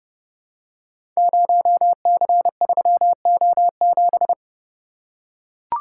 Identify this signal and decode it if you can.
Morse code, heard as a keyed tone, about 31 wpm, 700 Hz. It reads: 0C3O7
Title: Code Groups - 5 Letter:Numbers - 30wpm